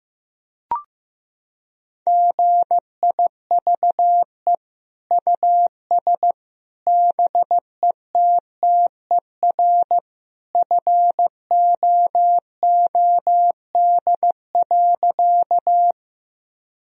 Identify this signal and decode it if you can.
Morse code, heard as a keyed tone, about 15 wpm, 700 Hz. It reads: GIVE US BETTER FOOD.
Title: Sentences from Top 1000 Words - Review Part 1 - 15wpm